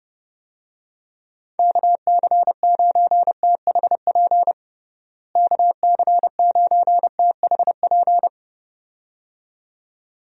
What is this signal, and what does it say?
Morse code, heard as a keyed tone, about 30 wpm, 700 Hz. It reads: KC9THP KC9THP
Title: USA Call Signs - 6 Characters - Repeated 2x - 30wpm